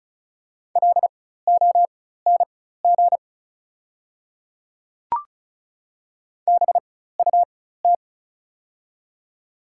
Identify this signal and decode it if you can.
Morse code, heard as a keyed tone, about 36 wpm, spaced as for 16 wpm, 700 Hz. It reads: LONG BUT
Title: 4x Character Spacing - Top 100 Words - 35wpm